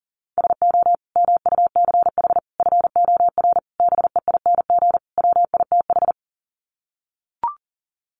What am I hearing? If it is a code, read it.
Morse code, heard as a keyed tone, about 40 wpm, 700 Hz. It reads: SO MUCH FOR BEING WITH
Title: Sets of 5 Words Encore 40wpm